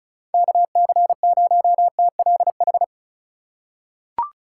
Morse code, heard as a keyed tone, about 36 wpm, 700 Hz. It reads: KC0TLH
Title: USA Call Signs - 3x Character Spacing - 35wpm